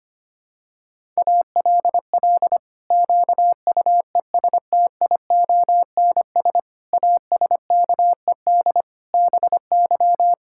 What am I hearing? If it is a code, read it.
Morse code, heard as a keyed tone, about 25 wpm, 700 Hz. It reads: ALL QUESTIONS ASKED BY
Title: Pangram Challenge v2 - 25wpm